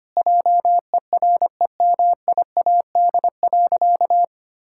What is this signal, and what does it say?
Morse code, heard as a keyed tone, about 25 wpm, 700 Hz. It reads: JEREMIAD.
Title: Pangram Challenge v2 - 25wpm